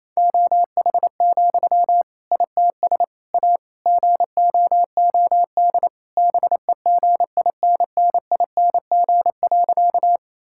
Morse code, heard as a keyed tone, about 28 wpm, 700 Hz. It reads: OH, ITS A GOOD BEGINNING.
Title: Sentences from Top 1000 Words - Review Part 5 - 28wpm